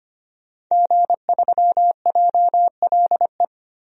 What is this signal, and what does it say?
Morse code, heard as a keyed tone, about 25 wpm, 700 Hz. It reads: G3JLE
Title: DXCC-100 Call Signs - 5 Characters - 25wpm